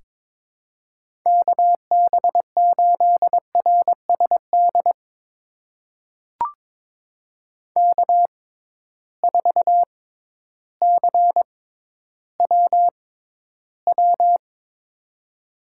Morse code, heard as a keyed tone, about 21 wpm, spaced as for 12 wpm, 700 Hz. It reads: KB8RSD K4CWW
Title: USA Call Signs - 6x Character Spacing - 22wpm